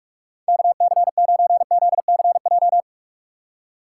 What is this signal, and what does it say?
Morse code, heard as a keyed tone, about 44 wpm, 700 Hz. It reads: KC9ZCJ
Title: USA Call Signs - 6 Characters - 6x Character Spacing - 45wpm